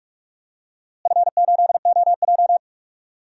Morse code, heard as a keyed tone, about 42 wpm, 700 Hz. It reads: F8OJ
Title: DXCC-100 Call Signs - 4 Characters - Repeated 3x - 45wpm